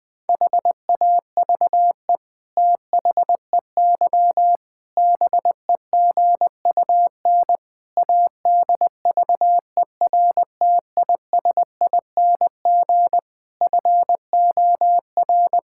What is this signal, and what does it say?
Morse code, heard as a keyed tone, about 21 wpm, 700 Hz. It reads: HAVE THEY BEGUN ADVERTISING FOR
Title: Sentences from Top 1000 Words - Review Part 6 - 20wpm